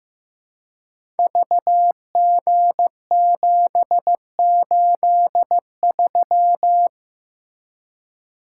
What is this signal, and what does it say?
Morse code, heard as a keyed tone, about 15 wpm, 700 Hz. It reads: VG783
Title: Code Groups - 5 Letter:Numbers - 15wpm